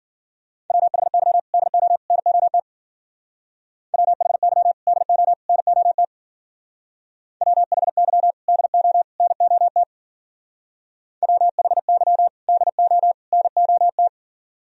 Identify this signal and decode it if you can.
Morse code, heard as a keyed tone, about 48 wpm, 700 Hz. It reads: WHY DO NOT WHY DO NOT WHY DO NOT WHY DO NOT
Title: Speed-Racing - Sets of 3 Words 40wpm